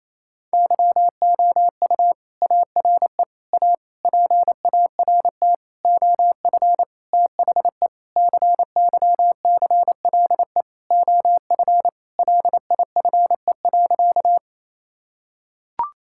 Morse code, heard as a keyed tone, about 27 wpm, 700 Hz. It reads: YOU ARE A PART OF THE CYCLE OF LIFE.
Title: Sentences from Top 1000 Words - Review Part 6 - 28wpm